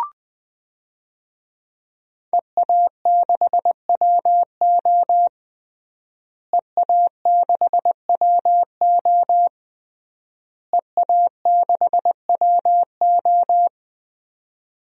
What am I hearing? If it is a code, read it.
Morse code, heard as a keyed tone, about 20 wpm, 700 Hz. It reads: EA6WO EA6WO EA6WO
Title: DXCC-100 Call Signs - 5 Characters - Repeated 3x - 20wpm